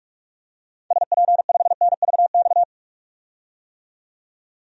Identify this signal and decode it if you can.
Morse code, heard as a keyed tone, about 46 wpm, 700 Hz. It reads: SP5NVX
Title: DXCC-100 Call Signs - 6 Characters - 45wpm